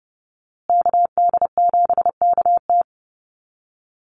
Morse code, heard as a keyed tone, about 30 wpm, 700 Hz. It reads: KD7KT
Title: USA Call Signs Repeated 2x 30wpm